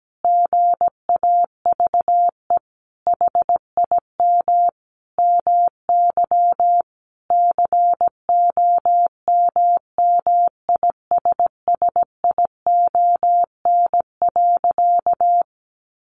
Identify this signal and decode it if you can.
Morse code, heard as a keyed tone, about 17 wpm, 700 Hz. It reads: GAVE HIM MY COMMISSION.
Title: Sentences from Top 1400 Words 17wpm